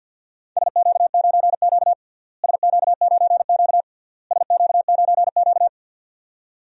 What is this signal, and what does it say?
Morse code, heard as a keyed tone, about 49 wpm, 700 Hz. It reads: SQ9Q SQ9Q SQ9Q
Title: DXCC-100 Call Signs - 4 Characters - Repeated 3x - 50wpm